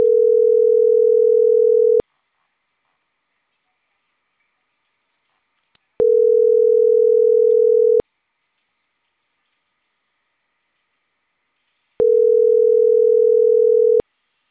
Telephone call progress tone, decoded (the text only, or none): ringback tone